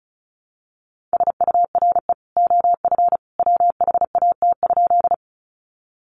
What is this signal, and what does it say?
Morse code, heard as a keyed tone, about 34 wpm, 700 Hz. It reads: SURE OF WHAT?
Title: Sentences from Top 300 Words 35wpm